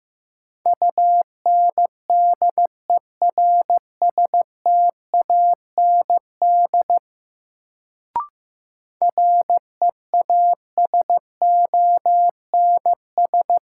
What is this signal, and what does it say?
Morse code, heard as a keyed tone, about 15 wpm, 700 Hz. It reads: UNDERSTAND REASONS
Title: Sets of 4 Words - New Word Drill - 15wpm